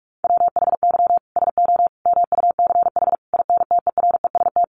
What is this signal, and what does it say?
Morse code, heard as a keyed tone, about 44 wpm, 700 Hz. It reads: WHY SO MUCH INTEREST
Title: Sentences from Top 1200 Words 45wpm